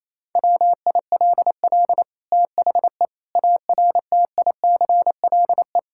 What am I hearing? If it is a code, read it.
Morse code, heard as a keyed tone, about 28 wpm, 700 Hz. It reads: WILL THE ARTICLE
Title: Sentences from Top 1000 Words - Review Part 5 - 28wpm